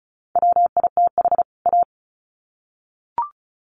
Morse code, heard as a keyed tone, about 34 wpm, 700 Hz. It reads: WITH A
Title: Sets of 2 Words 35wpm